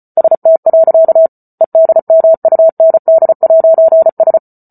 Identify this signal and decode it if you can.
Morse code, heard as a keyed tone, about 34 wpm, 650 Hz. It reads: ST. EDMUND'S